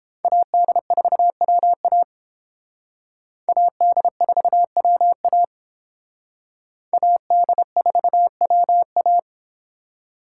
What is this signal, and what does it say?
Morse code, heard as a keyed tone, about 31 wpm, 700 Hz. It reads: AD4WA AD4WA AD4WA